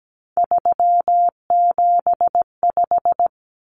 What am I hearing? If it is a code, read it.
Morse code, heard as a keyed tone, about 17 wpm, 700 Hz. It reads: 375
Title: Common Cut Numbers - 3-Digit 17wpm